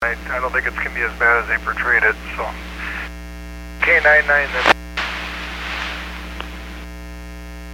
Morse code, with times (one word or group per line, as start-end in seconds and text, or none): none